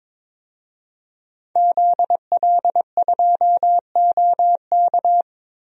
Morse code, heard as a keyed tone, about 22 wpm, 700 Hz. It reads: ZL2OK
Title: DXCC-100 Call Signs - 5 Characters - Repeated 3x - 22wpm